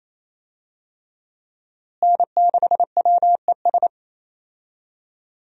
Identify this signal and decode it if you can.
Morse code, heard as a keyed tone, about 28 wpm, 700 Hz. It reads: N6WES
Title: USA Call Signs - 5 Characters - Repeated 3x - 28wpm